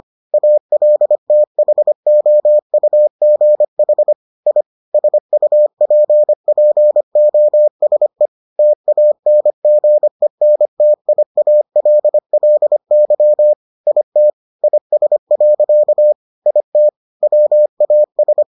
Morse code, heard as a keyed tone, about 25 wpm, 600 Hz. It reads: ALTHOUGH I SUPPOSE TANGENTIALLY IT IS. IT WAS